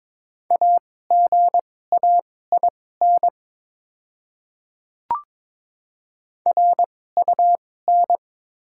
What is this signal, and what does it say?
Morse code, heard as a keyed tone, about 22 wpm, spaced as for 16 wpm, 700 Hz. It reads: AGAIN RUN
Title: 2x Character Spacing - Top 300 Words - 22wpm